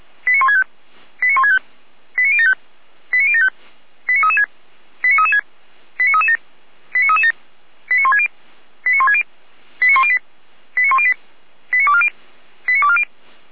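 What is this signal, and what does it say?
Five-tone selective call (ZVEI), heard as quick runs of five tones, tones of about 70 ms. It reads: 89165 89165 89075 89075 89336 89336 89337 89337 89160 89160 89108 89108 89240 89240
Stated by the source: ZVEI five-tone selective call received on an SDR, logged on the Feuerwehr Orlamünde fire brigade audio server.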